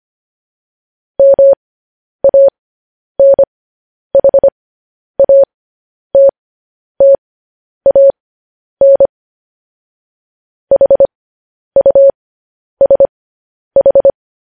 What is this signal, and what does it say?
Morse code, heard as a keyed tone, about 25 wpm, spaced as for 10 wpm, 550 Hz. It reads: MANHATTAN HUSH